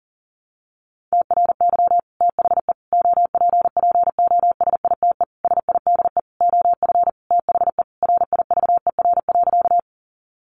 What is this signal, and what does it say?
Morse code, heard as a keyed tone, about 37 wpm, 700 Hz. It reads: TRY THE OPPOSITE SIDE OF THE RIVER.
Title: Sentences from Top 1300 Words 40wpm